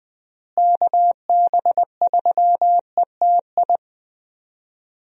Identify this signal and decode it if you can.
Morse code, heard as a keyed tone, about 20 wpm, 700 Hz. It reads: KB3ETI
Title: USA Call Signs - 6 Characters - Repeated 3x - 20wpm